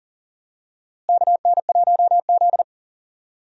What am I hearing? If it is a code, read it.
Morse code, heard as a keyed tone, about 40 wpm, 700 Hz. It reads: KN1Z
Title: USA Call Signs - 4 Characters - Repeated 3x - 40wpm